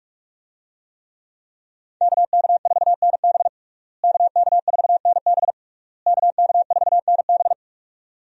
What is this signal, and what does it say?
Morse code, heard as a keyed tone, about 46 wpm, 700 Hz. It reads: KK4NB KK4NB KK4NB
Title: USA Call Signs - 5 Characters - Repeated 3x - 45wpm